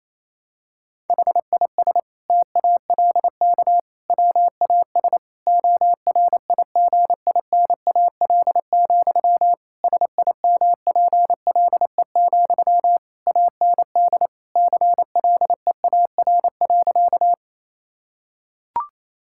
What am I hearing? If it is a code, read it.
Morse code, heard as a keyed tone, about 28 wpm, 700 Hz. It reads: HIS TALK WAS ORIGINAL, SIMPLE, AND CLEAR.
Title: Sentences from Top 1000 Words - Review Part 6 - 28wpm